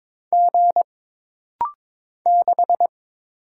Morse code, heard as a keyed tone, about 22 wpm, 700 Hz. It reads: G 6
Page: This rapid-fire timing is geared to push your ICR (Instant Character Recognition) skills.